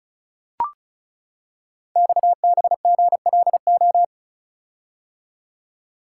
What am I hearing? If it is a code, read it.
Morse code, heard as a keyed tone, about 36 wpm, 700 Hz. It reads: XBGLO
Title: Code Groups - 5 Letter:Numbers - 35wpm